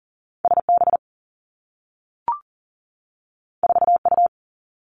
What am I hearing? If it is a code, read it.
Morse code, heard as a keyed tone, about 38 wpm, 700 Hz. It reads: S B 4 U
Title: Two Challenging Letter-Number Combinations 40wpm